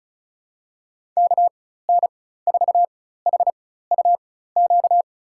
Morse code, heard as a keyed tone, about 36 wpm, spaced as for 16 wpm, 700 Hz. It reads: KN4HUQ
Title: USA Call Signs - 6 Characters - 4x Character Spacing - 35wpm